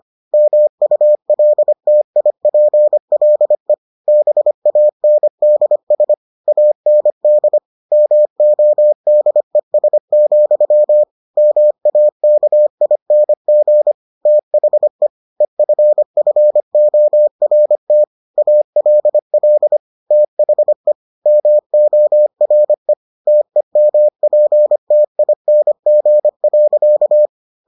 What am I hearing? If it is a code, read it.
Morse code, heard as a keyed tone, about 25 wpm, 600 Hz. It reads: MULTIPLE BANDS AND MODES, MAKING THE EFFORT ALL THE MORE TEMPTING.